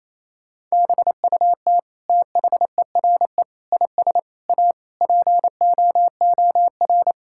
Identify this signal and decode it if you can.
Morse code, heard as a keyed tone, about 28 wpm, 700 Hz. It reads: BUT THERE IS A POOR
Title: Sentences from Top 1000 Words - Review Part 6 - 28wpm